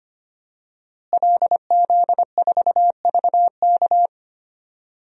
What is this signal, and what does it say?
Morse code, heard as a keyed tone, about 25 wpm, 700 Hz. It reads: LZ4VK
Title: DXCC-100 Call Signs - 5 Characters - Repeated 3x - 25wpm